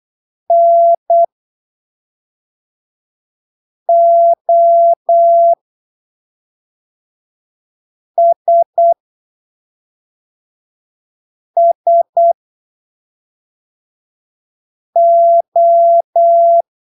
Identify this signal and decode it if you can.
Morse code, heard as a keyed tone, about 8 wpm, spaced as for 3 wpm, 680 Hz. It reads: NOSSO
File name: arquivos / texto / texto 22 08ppm